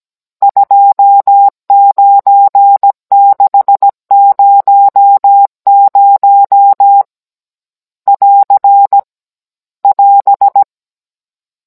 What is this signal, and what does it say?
Morse code, heard as a keyed tone, about 17 wpm, 800 Hz. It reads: 29600 <AR> <AS>
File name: Cours-NR19-16mm